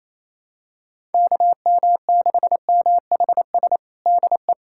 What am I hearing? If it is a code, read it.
Morse code, heard as a keyed tone, about 28 wpm, 700 Hz. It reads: KM6MHS DE
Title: USA Callsign DE Callsign - 6 Characters - 28wpm